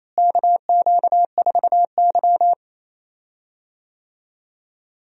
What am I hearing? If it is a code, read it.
Morse code, heard as a keyed tone, about 28 wpm, 700 Hz. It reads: KQ4Y
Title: USA Call Signs - 4 Characters - Repeated 2x - 28wpm